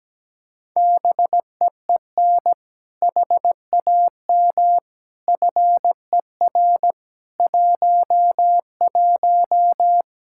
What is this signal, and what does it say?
Morse code, heard as a keyed tone, about 17 wpm, 700 Hz. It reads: BEEN HAM FER 11
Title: Practice QSO - Exchange 3 - More Info - 17wpm